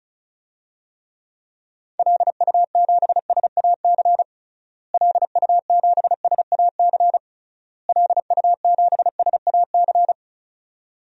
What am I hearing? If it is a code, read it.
Morse code, heard as a keyed tone, about 36 wpm, 700 Hz. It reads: LU7SAC LU7SAC LU7SAC
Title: DXCC-100 Call Signs - 6 Characters - Repeated 3x - 35wpm